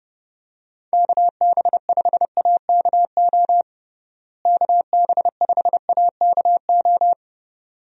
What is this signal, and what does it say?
Morse code, heard as a keyed tone, about 30 wpm, 700 Hz. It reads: KB5AKO KB5AKO
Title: USA Call Signs - 6 Characters - Repeated 2x - 30wpm